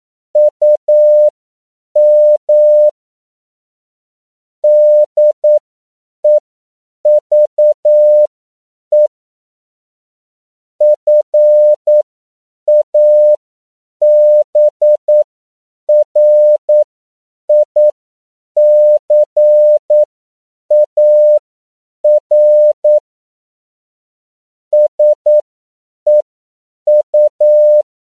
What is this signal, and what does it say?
Morse code, heard as a keyed tone, about 9 wpm, spaced as for 7 wpm, 600 Hz. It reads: UM DEVE FABRICAR SEU
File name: CW / 10PPM